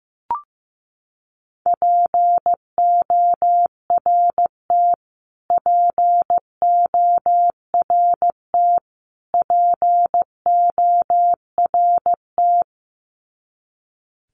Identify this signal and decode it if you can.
Morse code, heard as a keyed tone, about 15 wpm, 700 Hz. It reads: PORT PORT PORT
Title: Top 200 Words Repeated 15wpm